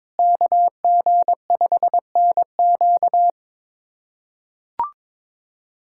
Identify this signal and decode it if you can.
Morse code, heard as a keyed tone, about 22 wpm, 700 Hz. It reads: KG5NQ
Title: USA Callsign DE Callsign - 5 Characters - 22wpm